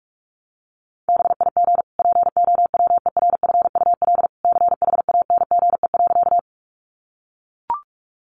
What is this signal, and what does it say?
Morse code, heard as a keyed tone, about 44 wpm, 700 Hz. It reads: BIG POWERFUL CHANGE.
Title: Sentences from Top 1300 Words 45wpm